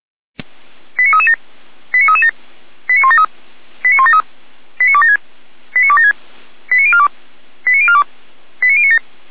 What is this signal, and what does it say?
Five-tone selective call (ZVEI), heard as quick runs of five tones, tones of about 80 ms. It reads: 89337 89337 89173 89173 89276 89276 89042 89042 89097